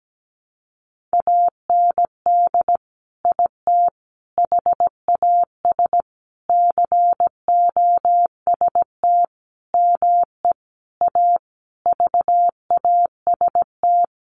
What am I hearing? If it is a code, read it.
Morse code, heard as a keyed tone, about 17 wpm, 700 Hz. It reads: AND IT HAS COST ME A VAST
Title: Sentences from Top 1300 Words 17wpm